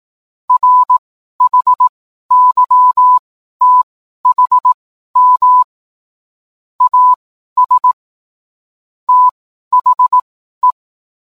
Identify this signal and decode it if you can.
Morse code, heard as a keyed tone, about 18 wpm, spaced as for 12 wpm, 1000 Hz. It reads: RHYTHM AS THE
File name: cwblog / cw mp3 directory / mem morse 12